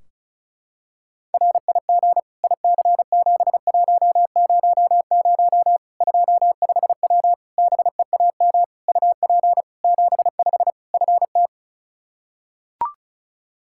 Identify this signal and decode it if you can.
Morse code, heard as a keyed tone, about 34 wpm, 700 Hz. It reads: RIG IC7100 25W BEAM UP 75 FT
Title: Practice QSO - Exchange 2 - Info - 35wpm